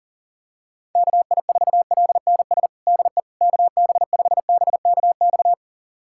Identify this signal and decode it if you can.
Morse code, heard as a keyed tone, about 40 wpm, 700 Hz. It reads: KI4LNS DE KB5BKX
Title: USA Callsign DE Callsign 40wpm